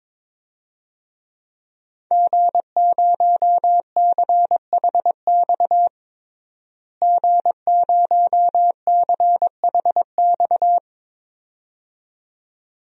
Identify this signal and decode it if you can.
Morse code, heard as a keyed tone, about 22 wpm, 700 Hz. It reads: G0CHX G0CHX
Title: DXCC-100 Call Signs - 5 Characters - Repeated 2x - 22wpm